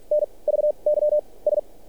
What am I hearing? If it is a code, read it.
Morse code, heard as a keyed tone, about 46 wpm, spaced as for 27 wpm, 600 Hz. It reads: NVYS
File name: archived / CRSnet